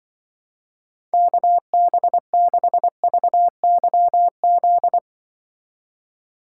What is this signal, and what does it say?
Morse code, heard as a keyed tone, about 24 wpm, 700 Hz. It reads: KB6VYZ
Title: Speed Racing - USA 2x3 Call Signs 20wpm